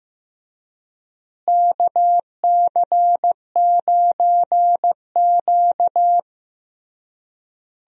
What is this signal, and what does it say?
Morse code, heard as a keyed tone, about 15 wpm, 700 Hz. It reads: KC9Q
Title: USA Call Signs - 4 Characters - Repeated 3x - 15wpm